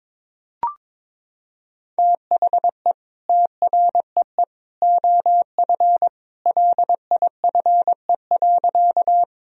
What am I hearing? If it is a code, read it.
Morse code, heard as a keyed tone, about 22 wpm, 700 Hz. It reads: THE TREE OF LIFE.
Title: Sentences from Top 1000 Words - Review Part 1 - 22wpm